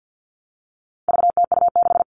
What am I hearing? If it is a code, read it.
Morse code, heard as a keyed tone, about 50 wpm, 700 Hz. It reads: VTU6
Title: All Cut Numbers - 4-Digit 50wpm